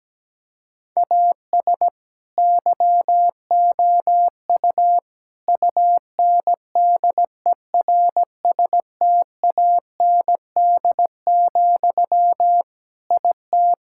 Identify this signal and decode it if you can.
Morse code, heard as a keyed tone, about 17 wpm, 700 Hz. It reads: AS YOU UNDERSTAND, IT
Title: Sentences from Top 1000 Words - Review Part 5 - 17wpm